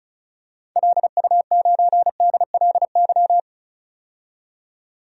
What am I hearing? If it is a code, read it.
Morse code, heard as a keyed tone, about 34 wpm, 700 Hz. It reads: LU9DLY
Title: DXCC-100 Call Signs - 6 Characters - 35wpm